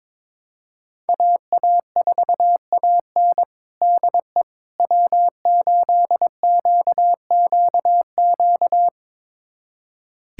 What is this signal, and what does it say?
Morse code, heard as a keyed tone, about 21 wpm, 700 Hz. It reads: AA4AN DE W8QQQ
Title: USA Callsign DE Callsign 22wpm